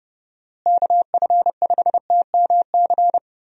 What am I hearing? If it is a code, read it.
Morse code, heard as a keyed tone, about 31 wpm, 700 Hz. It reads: KF5TMC
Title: USA Call Signs - 6 Characters - Repeated 3x - 30wpm